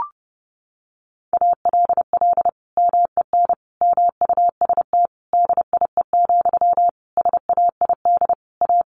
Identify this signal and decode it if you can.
Morse code, heard as a keyed tone, about 28 wpm, 700 Hz. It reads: ALL MEN MUST DIE, SAID A
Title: Sentences from Top 1300 Words 30wpm